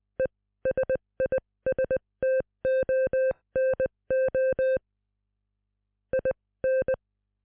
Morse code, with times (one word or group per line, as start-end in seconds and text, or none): 0.2-4.8 ESISTONO
6.1-6.9 IN